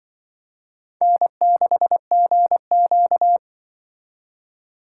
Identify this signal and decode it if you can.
Morse code, heard as a keyed tone, about 25 wpm, 700 Hz. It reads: N6GQ
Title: Speed Racing - USA 1x2 Call Signs 20wpm